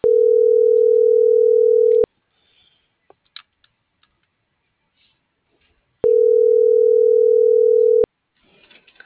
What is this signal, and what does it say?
Ringback tone (North America), 440 + 480 Hz.